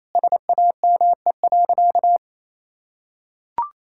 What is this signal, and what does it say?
Morse code, heard as a keyed tone, about 28 wpm, 700 Hz. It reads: SAME.
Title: Sentences from Top 1000 Words - Review Part 6 - 28wpm